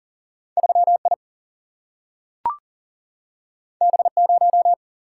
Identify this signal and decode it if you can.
Morse code, heard as a keyed tone, about 38 wpm, 700 Hz. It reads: 3I B0
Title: Single-Letter Single-Number 40wpm